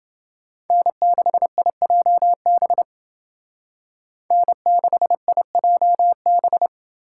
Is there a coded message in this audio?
Yes: Morse code from 0.7 to 6.7 s, heard as a keyed tone, about 30 wpm, 700 Hz. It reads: N6IJB N6IJB